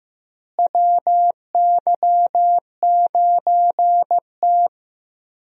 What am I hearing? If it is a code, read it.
Morse code, heard as a keyed tone, about 15 wpm, 700 Hz. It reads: WY9T
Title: USA Call Signs - 4 Characters - Repeated 3x - 15wpm